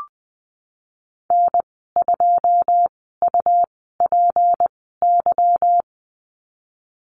Morse code, heard as a keyed tone, about 20 wpm, spaced as for 15 wpm, 700 Hz. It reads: N2UPY
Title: 154 Lesson 25g - Letter p - Call Signs 20wpm